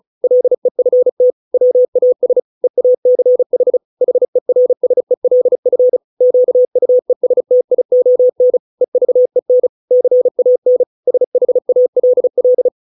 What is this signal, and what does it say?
Morse code, heard as a keyed tone, about 34 wpm, 500 Hz. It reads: LEFT WAS EACH HERSELF QUESTION EVEN CAN SHALL